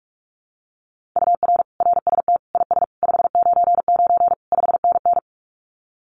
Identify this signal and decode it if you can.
Morse code, heard as a keyed tone, about 44 wpm, 700 Hz. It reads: UR RST IS 599 5NN
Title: Common QSO Phrases Encore 45wpm